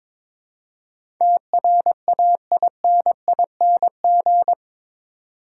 Morse code, heard as a keyed tone, about 22 wpm, 700 Hz. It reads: TRAINING
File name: practice / 22wpm / Eight Letter Words - 22wpm